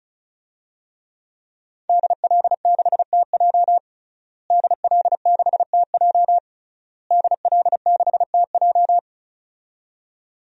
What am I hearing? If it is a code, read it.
Morse code, heard as a keyed tone, about 34 wpm, 690 Hz. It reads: DL6TJ DL6TJ DL6TJ
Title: DXCC-100 Call Signs - 5 Characters - Repeated 3x - 35wpm